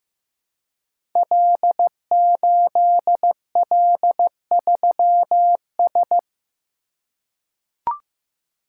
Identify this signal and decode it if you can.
Morse code, heard as a keyed tone, about 15 wpm, 700 Hz. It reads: L8L3S
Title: Code Groups - 5 Letter:Numbers - 15wpm